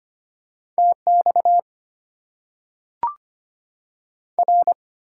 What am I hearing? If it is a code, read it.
Morse code, heard as a keyed tone, about 25 wpm, 700 Hz. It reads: TX R
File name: Top 90 QSO Elements - 8x Character Spacing - 25wpm